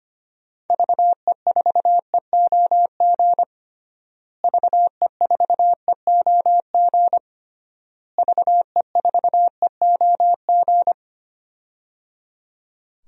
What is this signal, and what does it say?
Morse code, heard as a keyed tone, about 25 wpm, 700 Hz. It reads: VE4EOG VE4EOG VE4EOG
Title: DXCC-100 Call Signs - 6 Characters - Repeated 3x - 25wpm